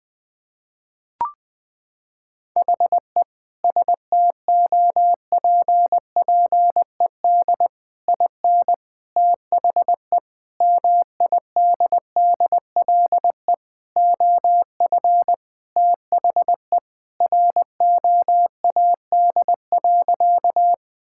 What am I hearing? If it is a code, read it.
Morse code, heard as a keyed tone, about 20 wpm, 700 Hz. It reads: HE STOPPED IN THE MIDDLE OF THE ROAD.
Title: Sentences with 3-Word Phrases - 20wpm